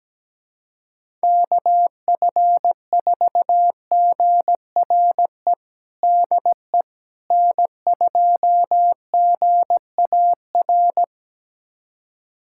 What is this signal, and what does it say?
Morse code, heard as a keyed tone, about 17 wpm, 700 Hz. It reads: KF4GRE DE N2GAR
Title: USA Callsign DE Callsign 17wpm